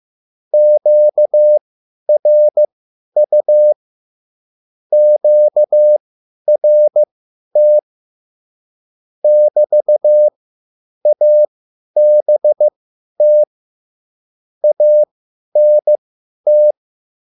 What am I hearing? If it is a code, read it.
Morse code, heard as a keyed tone, about 15 wpm, spaced as for 10 wpm, 600 Hz. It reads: QRU QRT <BT> ABT ANT
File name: QSL / CW lesson